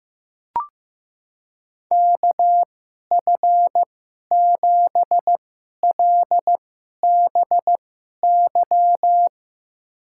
Morse code, heard as a keyed tone, about 15 wpm, spaced as for 11 wpm, 700 Hz. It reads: KF7LBY
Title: USA Call Signs - 6 Characters - 2x Character Spacing - 15wpm